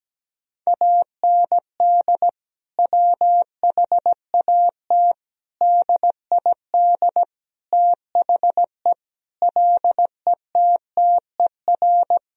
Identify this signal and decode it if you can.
Morse code, heard as a keyed tone, about 17 wpm, 700 Hz. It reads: AND WHAT DID THE LETTER
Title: Sentences from Top 1000 Words - Review Part 2 - 17wpm